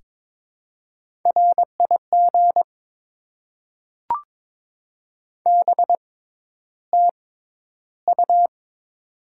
Morse code, heard as a keyed tone, about 22 wpm, spaced as for 11 wpm, 700 Hz. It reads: RIG BTU
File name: Top 20 QSO Elements - 6x Character Spacing - 22wpm